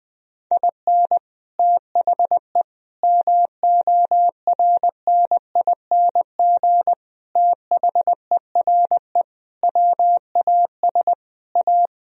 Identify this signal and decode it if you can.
Morse code, heard as a keyed tone, about 21 wpm, 700 Hz. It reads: IN THE MORNING THERE WAS A
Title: Sentences from Top 1000 Words - Review Part 1 - 20wpm